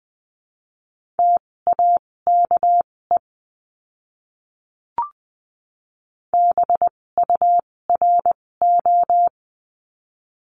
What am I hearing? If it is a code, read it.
Morse code, heard as a keyed tone, about 20 wpm, spaced as for 16 wpm, 700 Hz. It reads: TAKE BURO
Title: 256 Lesson 42a - Words 12-20wpm